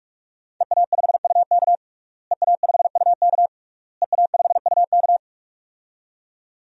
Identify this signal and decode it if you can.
Morse code, heard as a keyed tone, about 46 wpm, 700 Hz. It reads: EA5UK EA5UK EA5UK
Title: DXCC-100 Call Signs - 5 Characters - Repeated 3x - 45wpm